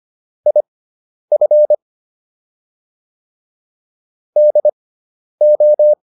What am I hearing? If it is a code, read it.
Morse code, heard as a keyed tone, about 25 wpm, spaced as for 10 wpm, 600 Hz. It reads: IF DO
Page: intermediate-practice-files